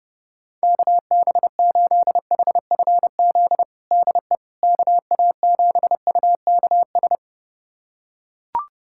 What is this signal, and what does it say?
Morse code, heard as a keyed tone, about 32 wpm, 700 Hz. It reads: KB8HFZ DE KA7UKS
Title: USA Callsign DE Callsign - 6 Characters - 30wpm